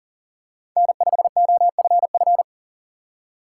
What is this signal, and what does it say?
Morse code, heard as a keyed tone, about 40 wpm, 700 Hz. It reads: N5OFF
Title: USA Call Signs - 6x Character Spacing - 40wpm